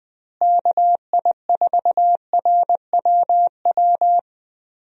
Morse code, heard as a keyed tone, about 20 wpm, 700 Hz. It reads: KI4RWW
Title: USA Call Signs - 6 Characters - Repeated 2x - 20wpm